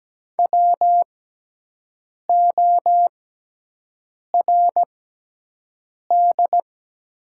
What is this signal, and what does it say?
Morse code, heard as a keyed tone, about 17 wpm, spaced as for 6 wpm, 700 Hz. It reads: WORD
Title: 6x Character Spacing - Top 100 Words - 17wpm